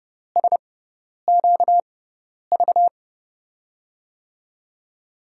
Morse code, heard as a keyed tone, about 30 wpm, spaced as for 10 wpm, 700 Hz. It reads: SQV
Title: USA Call Signs - 5 Characters - 6x Character Spacing - 30wpm